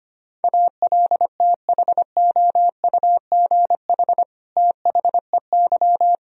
Morse code, heard as a keyed tone, about 25 wpm, 700 Hz. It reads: ALTHOUGH THEY
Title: Sentences from Top 1000 Words - Review Part 3 - 25wpm